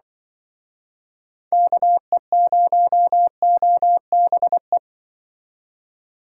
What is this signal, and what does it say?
Morse code, heard as a keyed tone, about 24 wpm, 700 Hz. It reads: KE0OBE
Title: Speed Racing - USA Call Signs 20wpm